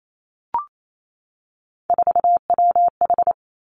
Morse code, heard as a keyed tone, about 28 wpm, 700 Hz. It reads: 4WH
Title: Three Challenging Letter-Number Combinations 28wpm